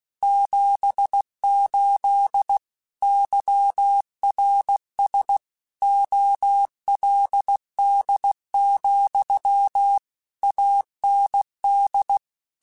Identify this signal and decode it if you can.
Morse code, heard as a keyed tone, about 16 wpm, 800 Hz. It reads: 78 YRS OLD, AND